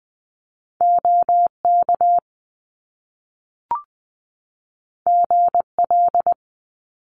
Morse code, heard as a keyed tone, about 20 wpm, 700 Hz. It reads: OK GL
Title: Top Two Letter Combinations in Words 20wpm